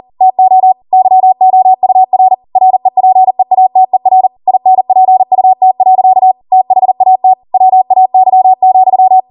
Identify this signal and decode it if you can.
Morse code, heard as a keyed tone, about 40 wpm, 750 Hz. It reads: TO YOUR REPEATER INPUT. THAT WAY,
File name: Morse / 40WPM